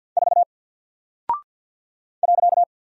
Morse code, heard as a keyed tone, about 50 wpm, 700 Hz. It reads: 4 .
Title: Single Character (Letter:Number:Punctuation:Prosign) 50wpm